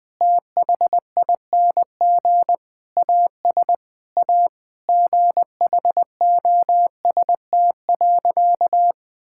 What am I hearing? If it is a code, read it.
Morse code, heard as a keyed tone, about 20 wpm, 700 Hz. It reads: THING AS A GHOST.
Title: Sentences with 8-Word Phrases - 20wpm